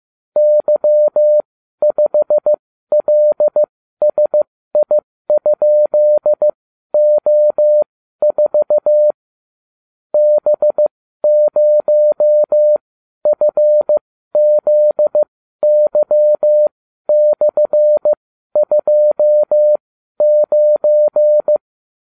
Morse code, heard as a keyed tone, about 15 wpm, spaced as for 12 wpm, 600 Hz. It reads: Y5LSI?O4 B0FZY/29